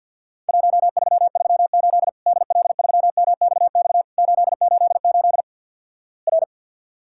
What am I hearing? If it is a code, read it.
Morse code, heard as a keyed tone, about 51 wpm, 700 Hz. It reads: 1338 DL3MXX 888 R
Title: Worked All Europe DX Contest - QTC Exchange with Repeats - 50wpm